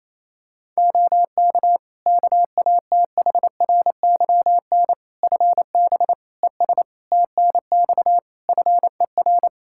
Morse code, heard as a keyed tone, about 28 wpm, 700 Hz. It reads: OK KATHRYN FB ES TNX FER
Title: K1USN SST QSO Practice 28wpm